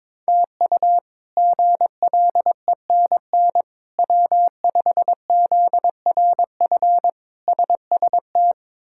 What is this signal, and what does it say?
Morse code, heard as a keyed tone, about 22 wpm, 700 Hz. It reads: TU GLENN W5ZRF SST
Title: K1USN SST QSO Practice - 22wpm